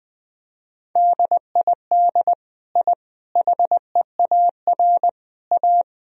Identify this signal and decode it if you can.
Morse code, heard as a keyed tone, about 21 wpm, 700 Hz. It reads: DID I HEAR A
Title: Sentences from Top 1000 Words - Review Part 2 - 20wpm